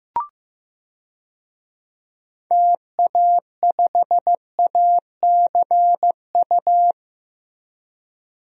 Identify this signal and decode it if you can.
Morse code, heard as a keyed tone, about 15 wpm, 700 Hz. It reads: TA5ACU
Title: DXCC-100 Call Signs - 6 Characters - Repeated 2x - 15wpm